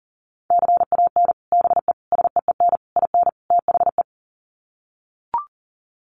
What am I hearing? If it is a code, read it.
Morse code, heard as a keyed tone, about 40 wpm, 700 Hz. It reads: CAN BE SEEN IN THE
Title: Sets of 5 Words Encore2 40wpm